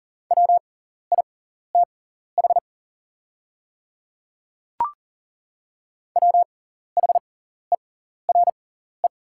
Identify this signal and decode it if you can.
Morse code, heard as a keyed tone, about 40 wpm, spaced as for 14 wpm, 700 Hz. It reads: WITH WHERE
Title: 6x Character Spacing - Top 300 Words - 40wpm